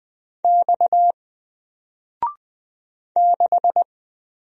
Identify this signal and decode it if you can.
Morse code, heard as a keyed tone, about 20 wpm, 700 Hz. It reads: X 6
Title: Single Letter-Number 20wpm